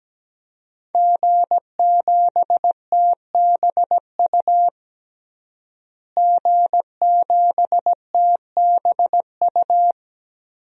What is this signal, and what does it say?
Morse code, heard as a keyed tone, about 17 wpm, 700 Hz. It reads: G7TBU G7TBU
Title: DXCC-100 Call Signs - 5 Characters - Repeated 3x - 17wpm